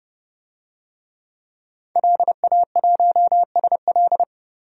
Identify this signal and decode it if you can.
Morse code, heard as a keyed tone, about 30 wpm, 700 Hz. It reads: LA1SL
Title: DXCC-100 Call Signs - 5 Characters - 30wpm